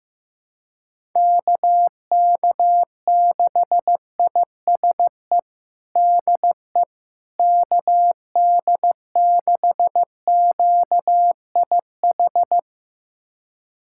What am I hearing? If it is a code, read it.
Morse code, heard as a keyed tone, about 15 wpm, 700 Hz. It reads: KK6ISE DE KD6QIH
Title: USA Callsign DE Callsign - 6 Characters - 15wpm